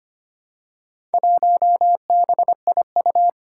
Morse code, heard as a keyed tone, about 25 wpm, 700 Hz. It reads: 1BIU
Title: Four Challenging Letter-Number Combinations 25wpm